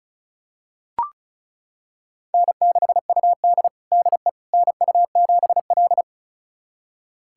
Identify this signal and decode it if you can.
Morse code, heard as a keyed tone, about 34 wpm, 700 Hz. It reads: N6UD DE NU7L
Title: USA Callsign DE Callsign - 4 Characters - 35wpm